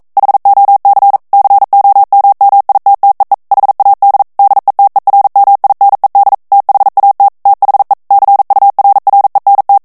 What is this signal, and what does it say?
Morse code, heard as a keyed tone, about 42 wpm, 800 Hz. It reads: HOC COMMITTEE HAD DETERMINED THAT THE CURRENT